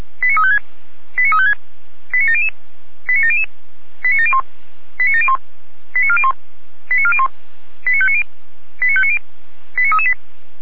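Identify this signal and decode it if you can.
Five-tone selective call (ZVEI), heard as quick runs of five tones, tones of about 70 ms. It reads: 89356 89356 89700 89700 89701 89701 89401 89401 89590 89590 89338
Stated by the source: ZVEI five-tone selective call received by SDR.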